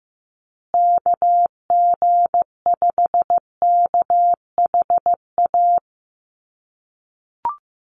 Morse code, heard as a keyed tone, about 15 wpm, 700 Hz. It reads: KG5KHA
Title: USA Call Signs Repeated 3x 15wpm